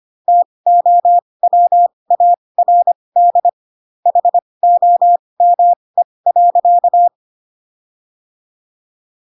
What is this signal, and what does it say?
Morse code, heard as a keyed tone, about 25 wpm, spaced as for 20 wpm, 700 Hz. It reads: TOWARD HOME.
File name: Pinocchio / Pinocchio ch2 15 25 ik1hge